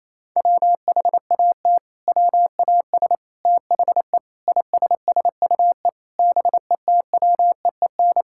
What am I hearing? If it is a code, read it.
Morse code, heard as a keyed tone, about 28 wpm, 690 Hz. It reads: WHAT WAS THE ISSUE BETWEEN
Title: Sentences from Top 1000 Words - Review Part 6 - 28wpm